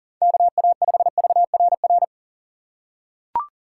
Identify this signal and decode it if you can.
Morse code, heard as a keyed tone, about 40 wpm, 700 Hz. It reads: KA5VRR
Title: USA Call Signs - 6 Characters - 2x Character Spacing - 40wpm